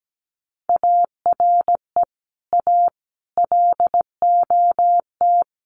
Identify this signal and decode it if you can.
Morse code, heard as a keyed tone, about 17 wpm, 700 Hz. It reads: ARE A LOT
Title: Sets of 5 Words 17wpm